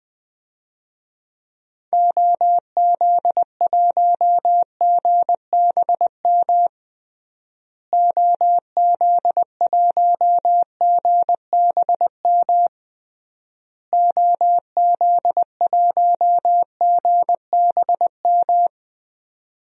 Morse code, heard as a keyed tone, about 20 wpm, 700 Hz. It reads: OZ1GBM OZ1GBM OZ1GBM
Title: DXCC-100 Call Signs - 6 Characters - Repeated 3x - 20wpm